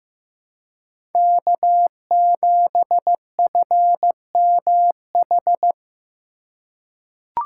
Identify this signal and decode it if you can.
Morse code, heard as a keyed tone, about 15 wpm, 700 Hz. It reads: K7FMH
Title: USA Call Signs - 5 Characters - 3x Character Spacing - 15wpm